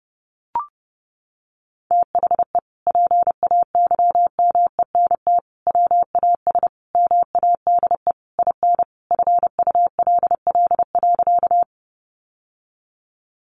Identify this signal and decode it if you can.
Morse code, heard as a keyed tone, about 30 wpm, 700 Hz. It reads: THE PAYMENT WAS MADE IN FULL.
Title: Sentences from Top 1200 Words 30wpm